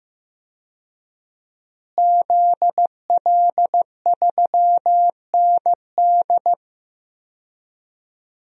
Morse code, heard as a keyed tone, about 15 wpm, 700 Hz. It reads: ZL3ND
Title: DXCC-100 Call Signs - 5 Characters - Repeated 3x - 15wpm